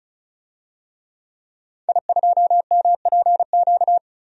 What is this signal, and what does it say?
Morse code, heard as a keyed tone, about 36 wpm, 700 Hz. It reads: I2MPQ
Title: DXCC-100 Call Signs - 5 Characters - 35wpm